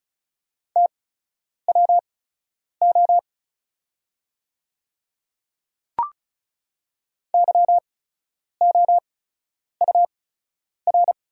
Morse code, heard as a keyed tone, about 36 wpm, spaced as for 10 wpm, 700 Hz. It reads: TWO YOUR
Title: 8x Character Spacing - Top 100 Words - 35wpm